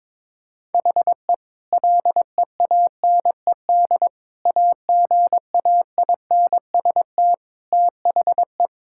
Morse code, heard as a keyed tone, about 22 wpm, 700 Hz. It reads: HE LEANED AGAINST THE
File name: practice / 22wpm / Sentences with 2-Word Phrases - 22wpm